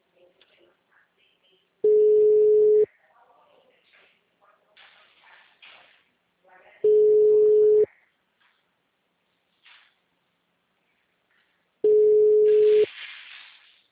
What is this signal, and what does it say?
Ringback tone (Europe, 425 Hz), with speech.